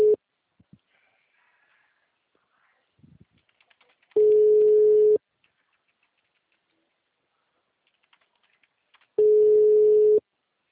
Ringback tone (Europe, 425 Hz).
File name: out- -2002- - - .121